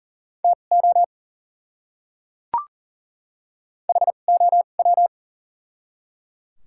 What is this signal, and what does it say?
Morse code, heard as a keyed tone, about 38 wpm, spaced as for 28 wpm, 700 Hz. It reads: TO HOW
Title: 2x Character Spacing - Top 100 Words - 40wpm